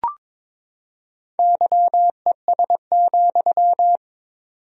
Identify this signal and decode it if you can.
Morse code, heard as a keyed tone, about 22 wpm, 700 Hz. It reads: YES,
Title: 4x Word-Spacing - Sentences from Top 500 Words - 22wpm